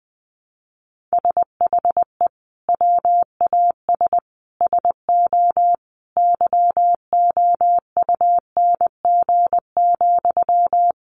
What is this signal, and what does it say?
Morse code, heard as a keyed tone, about 20 wpm, 700 Hz. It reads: SHE WAS SO YOUNG,
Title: Sentences from Top 400 Words 20wpm